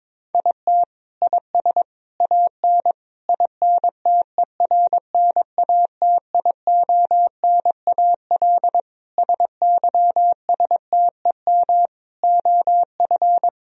Morse code, heard as a keyed tone, about 22 wpm, 700 Hz. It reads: IT IS AN INTERNATIONAL SYSTEM OF
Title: Sentences from Top 1000 Words - Review Part 4 - 22wpm